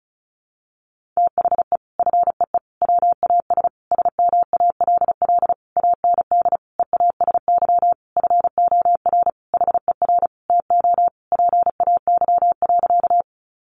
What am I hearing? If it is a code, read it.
Morse code, heard as a keyed tone, about 34 wpm, 700 Hz. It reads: THE FEE WAS SMALL AND EASY FOR HER TO PAY.
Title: Sentences from Top 1500 Words 35wpm